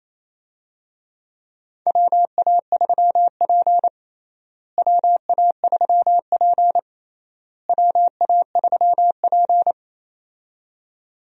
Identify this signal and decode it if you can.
Morse code, heard as a keyed tone, about 28 wpm, 700 Hz. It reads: WA3P WA3P WA3P
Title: USA Call Signs - 4 Characters - Repeated 3x - 28wpm